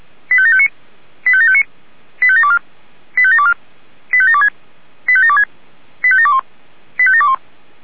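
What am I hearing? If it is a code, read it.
Five-tone selective call (ZVEI), heard as quick runs of five tones, tones of about 70 ms. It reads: 85759 85759 85724 85724 85726 85726 85721 85721
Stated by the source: Five-tone ZVEI selective call received on SDR, feuerwehr-orlamuende.org fire brigade audio server.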